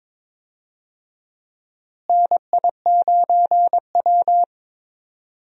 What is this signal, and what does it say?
Morse code, heard as a keyed tone, about 22 wpm, 700 Hz. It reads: NI9W
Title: USA Call Signs - 4 Characters - Repeated 3x - 22wpm